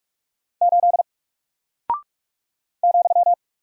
Morse code, heard as a keyed tone, about 44 wpm, 700 Hz. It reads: 8 ,
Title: Single Character (Letter:Number:Punctuation:Prosign) 45wpm